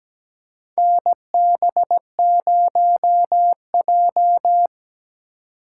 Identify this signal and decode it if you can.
Morse code, heard as a keyed tone, about 17 wpm, 700 Hz. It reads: NB0J
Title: USA Call Signs - 2x Character Spacing - 17wpm